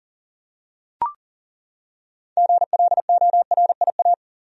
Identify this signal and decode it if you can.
Morse code, heard as a keyed tone, about 40 wpm, 700 Hz. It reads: GLORIA
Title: USA Common Names - 40wpm